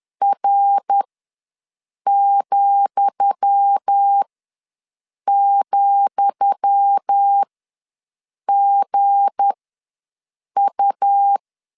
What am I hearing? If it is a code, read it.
Morse code, heard as a keyed tone, about 10 wpm, spaced as for 6 wpm, 800 Hz. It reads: R,,GU
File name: LEZIONI / 10 SIGNS / 10D COMA ws 10wpm 4314